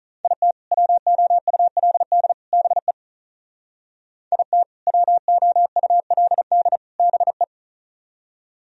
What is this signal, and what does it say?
Morse code, heard as a keyed tone, about 45 wpm, 700 Hz. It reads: IT WOULD BE IT WOULD BE